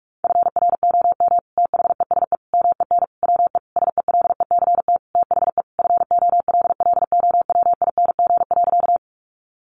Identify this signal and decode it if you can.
Morse code, heard as a keyed tone, about 46 wpm, 700 Hz. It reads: FROM THESE MEN WE SELECT THE FOLLOWING.
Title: Sentences from Top 1300 Words 45wpm